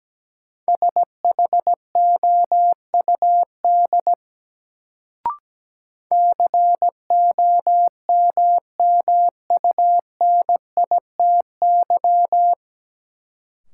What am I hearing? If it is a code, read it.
Morse code, heard as a keyed tone, about 17 wpm, 700 Hz. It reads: SHOUD COMMUNITY
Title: Sentences from Top 1000 Words - Review Part 4 - New Word Drill - 17wpm